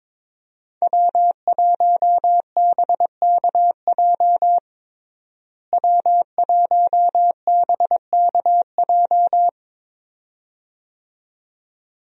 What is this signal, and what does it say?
Morse code, heard as a keyed tone, about 22 wpm, 700 Hz. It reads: W1BKJ W1BKJ
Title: USA Call Signs - 5 Characters - Repeated 2x - 22wpm